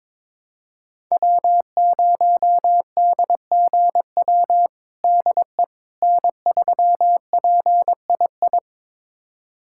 Morse code, heard as a keyed tone, about 22 wpm, 700 Hz. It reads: W0DGW DE N3PII
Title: USA Callsign DE Callsign - 5 Characters - 22wpm